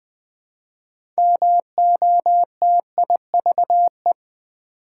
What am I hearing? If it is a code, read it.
Morse code, heard as a keyed tone, about 20 wpm, 700 Hz. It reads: MOTIVE
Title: Build A Word - 3 Letters At A Time - 6-Letter Words - 20wpm